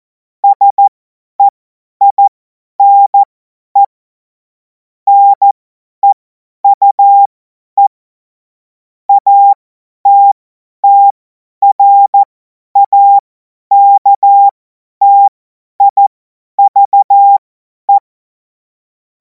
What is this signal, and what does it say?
Morse code, heard as a keyed tone, about 14 wpm, spaced as for 10 wpm, 800 Hz. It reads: SEINE NEUE ATTRAKTIVE
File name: Morsekurs / Witze / Witz 14. Juli 2026 50 F70